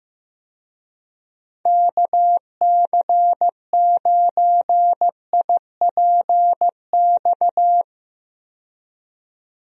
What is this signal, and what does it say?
Morse code, heard as a keyed tone, about 15 wpm, 700 Hz. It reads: KC9IPX
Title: USA Call Signs - 6 Characters - 15wpm